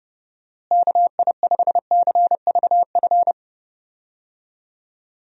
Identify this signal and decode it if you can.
Morse code, heard as a keyed tone, about 30 wpm, 700 Hz. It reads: KI5CVF
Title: USA Call Signs - 6 Characters - Repeated 2x - 30wpm